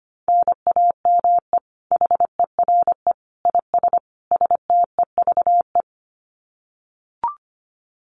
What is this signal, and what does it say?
Morse code, heard as a keyed tone, about 25 wpm, 700 Hz. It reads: NAME HERE IS STEVE
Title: Common QSO Phrases 25wpm